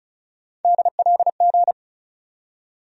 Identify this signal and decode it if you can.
Morse code, heard as a keyed tone, about 34 wpm, 700 Hz. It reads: DLG
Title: USA Call Sign Postfixes - 3 Letters 35wpm